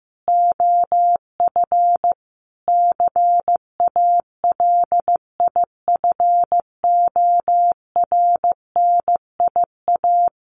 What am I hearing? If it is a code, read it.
Morse code, heard as a keyed tone, about 15 wpm, 700 Hz. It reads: OF CALIFORNIA
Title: Sets of 5 Words Encore2 15wpm